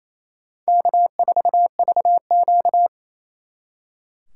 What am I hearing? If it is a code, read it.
Morse code, heard as a keyed tone, about 35 wpm, 700 Hz. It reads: K4VQ
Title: Speed Racing - USA 1x2 Call Signs 28wpm